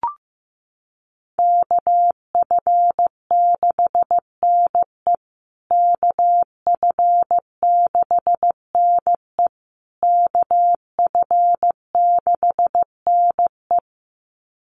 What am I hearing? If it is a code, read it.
Morse code, heard as a keyed tone, about 15 wpm, 700 Hz. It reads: KF6NE KF6NE KF6NE
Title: USA Call Signs Repeated 3x 15wpm